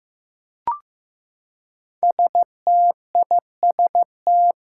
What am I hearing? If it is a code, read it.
Morse code, heard as a keyed tone, about 15 wpm, 700 Hz. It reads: STIST
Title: Five Challenging Letter-Number Combinations 15wpm